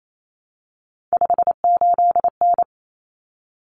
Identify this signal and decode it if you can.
Morse code, heard as a keyed tone, about 28 wpm, 700 Hz. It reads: 58N
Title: Common Cut Numbers - 3-Digit 28wpm